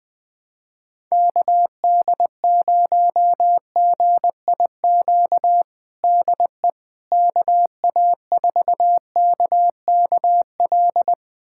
Morse code, heard as a keyed tone, about 20 wpm, 700 Hz. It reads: KD0GIQ DE KA4KKL
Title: USA Callsign DE Callsign - 6 Characters - 20wpm